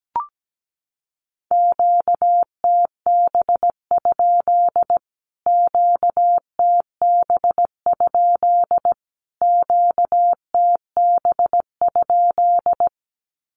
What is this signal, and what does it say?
Morse code, heard as a keyed tone, about 17 wpm, 700 Hz. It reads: QTB? QTB? QTB?
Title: Q-Codes Repeated 3x 17wpm